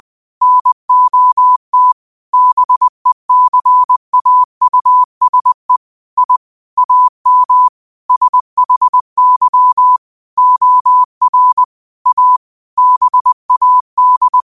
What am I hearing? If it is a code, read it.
Morse code, heard as a keyed tone, about 20 wpm, 1000 Hz. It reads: NOT BECAUSE I AM SHY OR A BAD
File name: qotdfiles / 20wpm